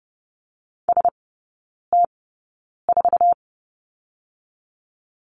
Morse code, heard as a keyed tone, about 30 wpm, spaced as for 9 wpm, 700 Hz. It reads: ST4
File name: Lesson 18e - Letter m - Three Characters 30wpm